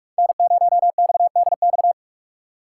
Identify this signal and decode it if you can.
Morse code, heard as a keyed tone, about 59 wpm, spaced as for 52 wpm, 700 Hz. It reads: N0XDX
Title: Speed Racing - USA Call Signs 45wpm